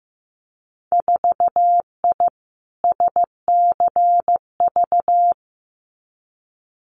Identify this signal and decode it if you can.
Morse code, heard as a keyed tone, about 15 wpm, 700 Hz. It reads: 4I SCV
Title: Winter Field Day Exchanges Encore 15wpm